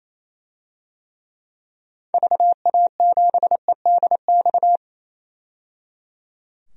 Morse code, heard as a keyed tone, about 27 wpm, 700 Hz. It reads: VA7EDX
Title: DXCC-100 Call Signs - 6 Characters - 28wpm